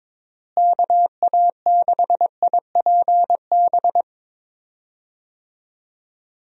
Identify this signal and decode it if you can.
Morse code, heard as a keyed tone, about 22 wpm, 700 Hz. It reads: KA6IPB
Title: USA Call Signs - 6 Characters - Repeated 2x - 22wpm